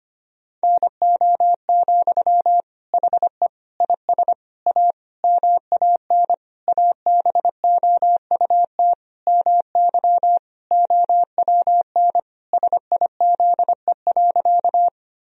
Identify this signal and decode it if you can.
Morse code, heard as a keyed tone, about 25 wpm, 700 Hz. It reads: NO, HE IS A MAN ABOUT MY OWN SIZE.
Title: Sentences from Top 1000 Words - Review Part 2 - 25wpm